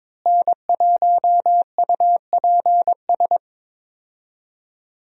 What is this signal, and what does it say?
Morse code, heard as a keyed tone, about 22 wpm, 700 Hz. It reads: N1UPS
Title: USA Call Signs - 5 Characters - Repeated 3x - 22wpm